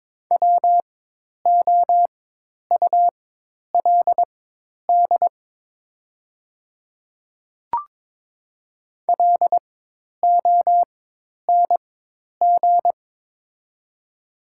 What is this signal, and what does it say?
Morse code, heard as a keyed tone, about 22 wpm, spaced as for 10 wpm, 700 Hz. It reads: WOULD LONG
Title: 4x Character Spacing - Top 100 Words - 22wpm